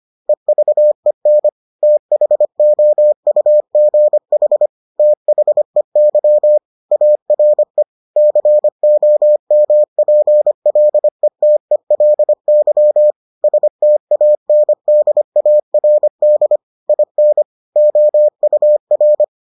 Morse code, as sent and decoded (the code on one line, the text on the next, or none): . ...- . -. / - .... --- ..- --. .... / - .... -.-- / .- .-. / -.-. --- -- .--. .-.. - . .-.. -.-- / ... - .- -. -.. .- .-. -.. / .. -. / --- ..- .-.
EVEN THOUGH THEY ARE COMPLETELY STANDARD IN OUR